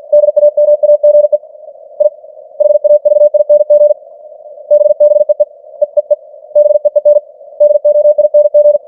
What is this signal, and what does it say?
Morse code, heard as a keyed tone, about 50 wpm, 590 Hz. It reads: DAMAGE I HAVING BBEE EEE BEEN DOING
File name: archived / CWBros